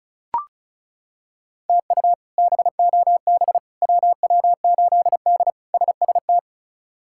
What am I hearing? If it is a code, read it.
Morse code, heard as a keyed tone, about 36 wpm, 700 Hz. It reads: TU BOB WW8D SST
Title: K1USN SST QSO Practice - 35wpm